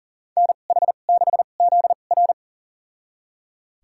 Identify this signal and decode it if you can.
Morse code, heard as a keyed tone, about 40 wpm, spaced as for 29 wpm, 700 Hz. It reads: NH6ZR
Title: USA Call Signs - 5 Characters - 2x Character Spacing - 40wpm